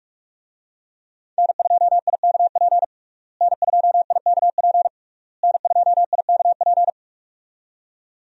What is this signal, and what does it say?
Morse code, heard as a keyed tone, about 46 wpm, 700 Hz. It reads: N2IKP N2IKP N2IKP
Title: USA Call Signs - 5 Characters - Repeated 3x - 45wpm